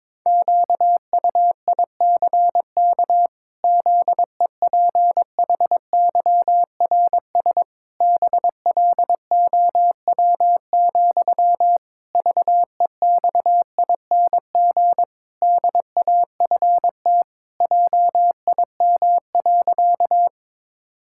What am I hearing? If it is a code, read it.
Morse code, heard as a keyed tone, about 22 wpm, 700 Hz. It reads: QUICK ZEPHYRS BLOW, VEXING DAFT JIM.
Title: Pangram Challenge v2 - 22wpm